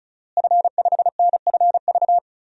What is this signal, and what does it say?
Morse code, heard as a keyed tone, about 36 wpm, 700 Hz. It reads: F5NFV
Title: DXCC-100 Call Signs - 5 Characters - Repeated 3x - 35wpm